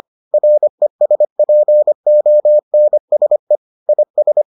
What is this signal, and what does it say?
Morse code, heard as a keyed tone, about 25 wpm, 600 Hz. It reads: RESPONSE IS